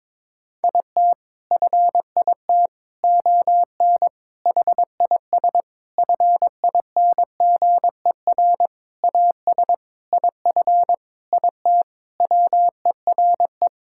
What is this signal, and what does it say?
Morse code, heard as a keyed tone, about 22 wpm, 700 Hz. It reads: IT FIT ON HIS FINGER AS IF IT WERE
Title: Sentences from Top 1600 Words 22wpm